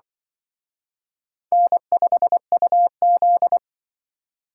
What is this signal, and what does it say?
Morse code, heard as a keyed tone, about 24 wpm, 700 Hz. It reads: N5UZ
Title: Speed Racing - USA 1x2 Call Signs 20wpm